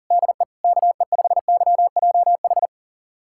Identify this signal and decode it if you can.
Morse code, heard as a keyed tone, about 40 wpm, 700 Hz. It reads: DE KE5YJH
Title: USA Callsign DE Callsign 40wpm